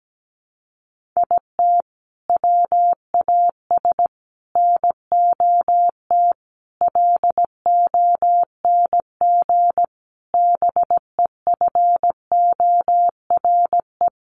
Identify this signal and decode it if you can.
Morse code, heard as a keyed tone, about 17 wpm, 700 Hz. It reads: IT WAS NOT LONG BEFORE
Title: Sets of 5 Words Encore2 17wpm